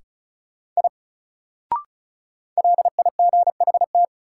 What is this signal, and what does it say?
Morse code, heard as a keyed tone, about 34 wpm, 700 Hz. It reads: I LIGHT
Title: Sentences from Top 200 Words - New Word Drill - 35wpm